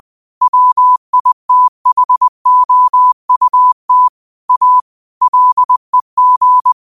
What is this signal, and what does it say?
Morse code, heard as a keyed tone, about 20 wpm, 1000 Hz. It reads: WITHOUT A LEG
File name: qotdfiles / 20wpm